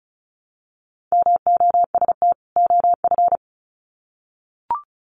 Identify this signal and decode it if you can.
Morse code, heard as a keyed tone, about 34 wpm, 700 Hz. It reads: MOST OF
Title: Sets of 2 Words 35wpm